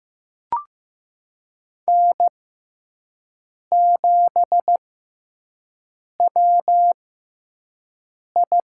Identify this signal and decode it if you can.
Morse code, heard as a keyed tone, about 16 wpm, spaced as for 5 wpm, 700 Hz. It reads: N7WI
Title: USA Call Signs - 5 Characters - 6x Character Spacing - 15wpm